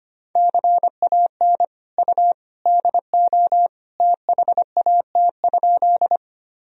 Morse code, heard as a keyed tone, about 25 wpm, 700 Hz. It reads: CAN U DO THAT?
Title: Sentences from Top 100 Words and Abbreviations 25wpm